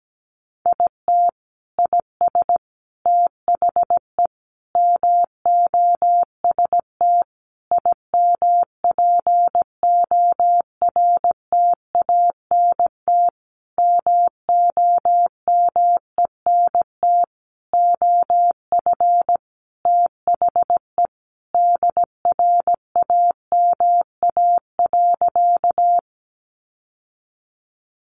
Morse code, heard as a keyed tone, about 17 wpm, 700 Hz. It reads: IT IS THE MOST IMPORTANT MOMENT OF THE DRAMA.
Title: Sentences from Top 1500 Words 17wpm